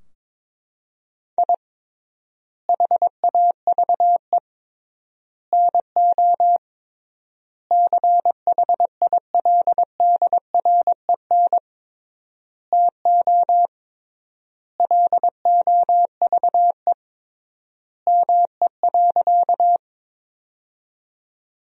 Morse code, heard as a keyed tone, about 22 wpm, 700 Hz. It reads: I HAVE NO CHILDREN TO LOVE ME.
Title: 3x Word-Spacing - Sentences from Top 400 Words - 22wpm